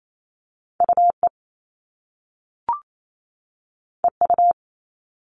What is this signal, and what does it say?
Morse code, heard as a keyed tone, about 28 wpm, 700 Hz. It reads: UE EU
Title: All Cut Numbers - 2-Digit 28wpm